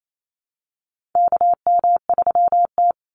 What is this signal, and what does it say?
Morse code, heard as a keyed tone, about 28 wpm, 700 Hz. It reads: KM3T
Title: Call Signs Repeated 2x - 28wpm